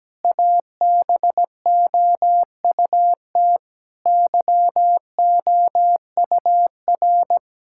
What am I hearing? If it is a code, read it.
Morse code, heard as a keyed tone, about 17 wpm, 700 Hz. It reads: ABOUT YOUR
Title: Sentences from Top 1000 Words - Review Part 5 - 17wpm